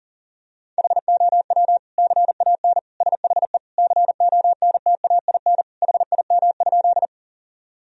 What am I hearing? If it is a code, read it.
Morse code, heard as a keyed tone, about 40 wpm, 700 Hz. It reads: HOW CAN SHE CONTAIN HIM?
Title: Sentences from Top 1000 Words - Review Part 2 - 40wpm